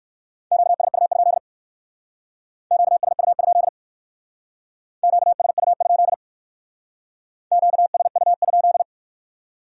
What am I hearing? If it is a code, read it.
Morse code, heard as a keyed tone, about 57 wpm, 700 Hz. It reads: QSU? QSU? QSU? QSU?